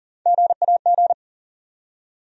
Morse code, heard as a keyed tone, about 39 wpm, 700 Hz. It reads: GAG
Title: USA Call Sign Postfixes - 3 Letters 40wpm